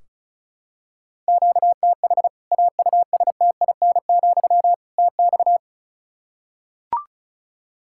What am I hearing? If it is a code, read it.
Morse code, heard as a keyed tone, about 36 wpm, 700 Hz. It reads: QTH AUSTIN, TX
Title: Practice QSO - Exchange 1 - Beginning - 35wpm